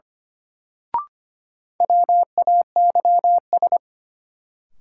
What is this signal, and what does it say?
Morse code, heard as a keyed tone, about 25 wpm, 700 Hz. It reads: WAYS
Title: Sets of 5 Words Encore - New Word Drill - 25wpm